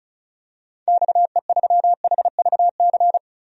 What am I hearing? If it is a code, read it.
Morse code, heard as a keyed tone, about 36 wpm, 700 Hz. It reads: XE3HVC
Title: DXCC-100 Call Signs - 6 Characters - 35wpm